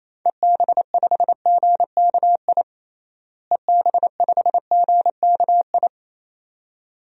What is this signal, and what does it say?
Morse code, heard as a keyed tone, about 28 wpm, 700 Hz. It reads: EB5GKI EB5GKI
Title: DXCC-100 Call Signs - 6 Characters - Repeated 2x - 28wpm